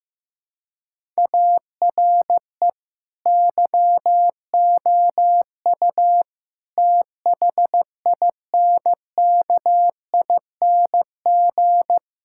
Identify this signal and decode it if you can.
Morse code, heard as a keyed tone, about 16 wpm, 700 Hz. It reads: ARE YOU THINKING
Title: Sentences from Top 1000 Words - Review Part 6 - 15wpm